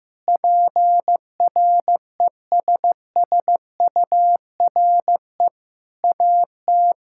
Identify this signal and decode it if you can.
Morse code, heard as a keyed tone, about 16 wpm, 700 Hz. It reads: PRESSURE AT
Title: Sentences from Top 1000 Words - Review Part 6 - 15wpm